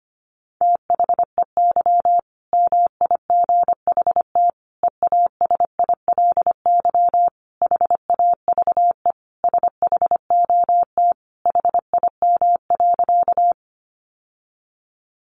Morse code, heard as a keyed tone, about 25 wpm, 700 Hz. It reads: THEY MIGHT EASILY HAVE SHOT HIM.
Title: Sentences from Top 1100 Words 25wpm